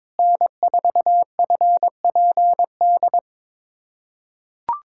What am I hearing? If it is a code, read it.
Morse code, heard as a keyed tone, about 22 wpm, 700 Hz. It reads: N4FPD
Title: USA Callsign DE Callsign - 5 Characters - 22wpm